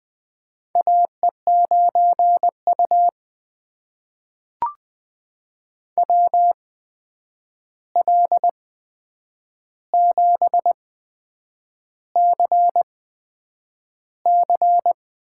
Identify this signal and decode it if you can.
Morse code, heard as a keyed tone, about 20 wpm, spaced as for 8 wpm, 700 Hz. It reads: AE9U WL7CC
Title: USA Call Signs - 8x Character Spacing - 20wpm